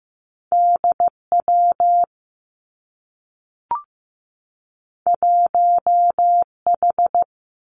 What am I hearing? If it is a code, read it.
Morse code, heard as a keyed tone, about 15 wpm, 700 Hz. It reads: D W 1 H
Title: Two Challenging Letter-Number Combinations 15wpm